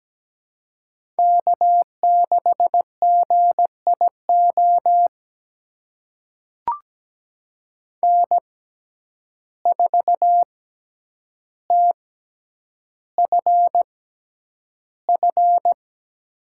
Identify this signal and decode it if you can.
Morse code, heard as a keyed tone, about 17 wpm, spaced as for 9 wpm, 700 Hz. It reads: K6GIO N4TFF
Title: USA Call Signs - 5 Characters - 6x Character Spacing - 17wpm